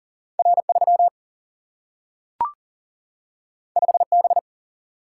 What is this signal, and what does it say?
Morse code, heard as a keyed tone, about 40 wpm, 700 Hz. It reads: R 3 5 B
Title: Single-Letter Single-Number 40wpm